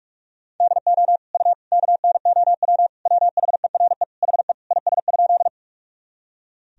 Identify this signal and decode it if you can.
Morse code, heard as a keyed tone, about 41 wpm, 700 Hz. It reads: DO U KNOW WHERE HE IS?
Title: Sentences from Top 300 Words and Abbreviations 45wpm